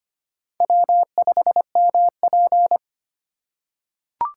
Morse code, heard as a keyed tone, about 25 wpm, 690 Hz. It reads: W5MP
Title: USA Call Signs - 4 Characters - 4x Character Spacing - 25wpm